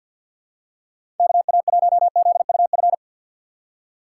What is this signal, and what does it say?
Morse code, heard as a keyed tone, about 52 wpm, 700 Hz. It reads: KA1ZUF
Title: USA Call Signs - 6 Characters - 8x Character Spacing - 50wpm